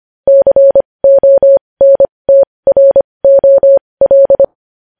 Morse code, heard as a keyed tone, about 25 wpm, spaced as for 20 wpm, 550 Hz. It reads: CONTROL
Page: news headlines in morse code at 20 wpm